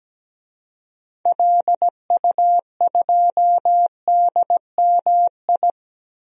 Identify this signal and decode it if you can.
Morse code, heard as a keyed tone, about 17 wpm, 700 Hz. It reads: LU2DMI
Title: DXCC-100 Call Signs - 6 Characters - Repeated 3x - 17wpm